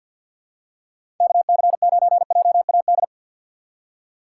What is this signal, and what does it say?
Morse code, heard as a keyed tone, about 46 wpm, 690 Hz. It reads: KC9JAD
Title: USA Call Signs - 6x Character Spacing - 50wpm